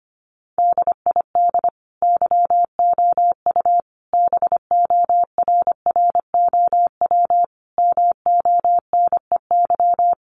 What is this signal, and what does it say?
Morse code, heard as a keyed tone, about 25 wpm, 700 Hz. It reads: DID YOU BORROW MONEY
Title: Sentences from Top 1500 Words 25wpm